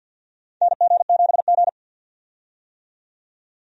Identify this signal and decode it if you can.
Morse code, heard as a keyed tone, about 50 wpm, 700 Hz. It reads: NG7G
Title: USA Call Signs - 4 Characters - 50wpm